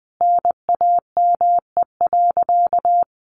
Morse code, heard as a keyed tone, about 19 wpm, 700 Hz. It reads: NAME.
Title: Sentences from Top 200 Words 20wpm